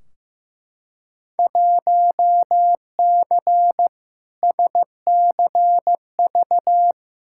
Morse code, heard as a keyed tone, about 15 wpm, 700 Hz. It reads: 1C SCV
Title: Field Day Exchanges - 15wpm